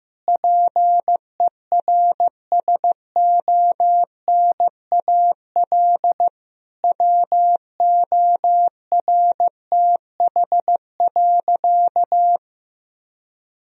Morse code, heard as a keyed tone, about 15 wpm, 700 Hz. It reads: PERSONAL WORTH.
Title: Sentences from Top 1000 Words - Review Part 4 - 15wpm